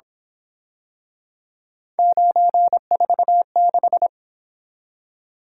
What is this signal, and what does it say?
Morse code, heard as a keyed tone, about 27 wpm, 700 Hz. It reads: 946
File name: Speed-Racing - Three Letter-Number Combinations - 22wpm